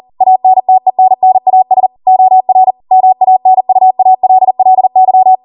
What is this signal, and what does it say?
Morse code, heard as a keyed tone, about 40 wpm, 750 Hz. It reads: ANTENNAS OR MANUALLY